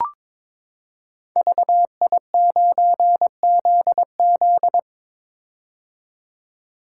Morse code, heard as a keyed tone, about 22 wpm, 700 Hz. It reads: VI9ZZ
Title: Code Groups - 5 Letter:Numbers - 22wpm